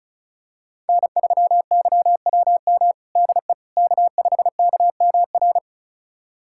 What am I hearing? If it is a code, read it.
Morse code, heard as a keyed tone, about 36 wpm, 700 Hz. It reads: N3YWM DE K5KMR
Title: USA Callsign DE Callsign - 5 Characters - 35wpm